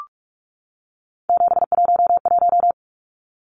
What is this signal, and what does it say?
Morse code, heard as a keyed tone, about 44 wpm, 700 Hz. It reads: 711
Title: Numbers - 3-Digit 45wpm